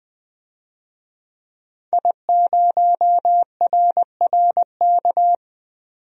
Morse code, heard as a keyed tone, about 20 wpm, 700 Hz. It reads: I0RRK
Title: DXCC-100 Call Signs - 5 Characters - 20wpm